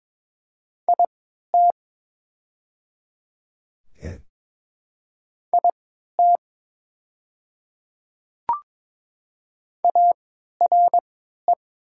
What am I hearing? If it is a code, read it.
Morse code, heard as a keyed tone, about 22 wpm, spaced as for 13 wpm, 700 Hz. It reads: IT IT ARE
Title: 3x Character Spacing - Top 200 Words - 22wpm